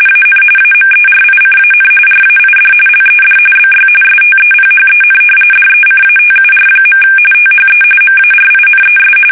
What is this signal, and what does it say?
RTTY, about 45.45 baud. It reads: AN AMERICAN RAID LED BY STEPHEN DECATUR. THE MARINES
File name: fsk850 / usnavy1